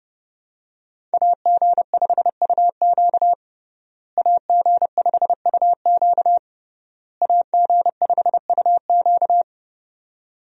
Morse code, heard as a keyed tone, about 32 wpm, 700 Hz. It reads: AG5UQ AG5UQ AG5UQ
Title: USA Call Signs - 5 Characters - Repeated 3x - 30wpm